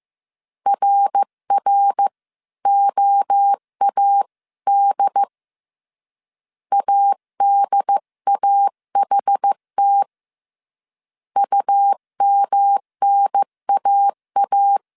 Morse code, heard as a keyed tone, about 15 wpm, 800 Hz. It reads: RROAD ADAHT UMNAA
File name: CW / DET / 1 3 1500C S C 14wpm 2458